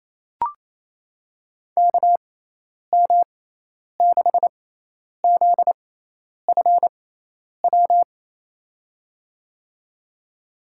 Morse code, heard as a keyed tone, about 27 wpm, spaced as for 10 wpm, 700 Hz. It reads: KM6ZFW
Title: USA Call Signs - 6 Characters - 6x Character Spacing - 28wpm